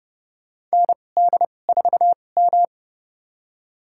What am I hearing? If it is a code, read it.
Morse code, heard as a keyed tone, about 30 wpm, spaced as for 22 wpm, 700 Hz. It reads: ND4M
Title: USA Call Signs - 2x Character Spacing - 30wpm